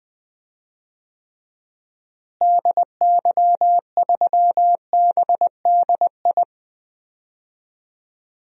Morse code, heard as a keyed tone, about 19 wpm, 700 Hz. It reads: DY3BDI
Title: DXCC-100 Call Signs - 6 Characters - Repeated 3x - 20wpm